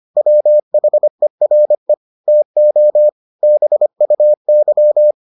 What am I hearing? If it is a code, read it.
Morse code, heard as a keyed tone, about 25 wpm, 600 Hz. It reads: WHERE TO BUY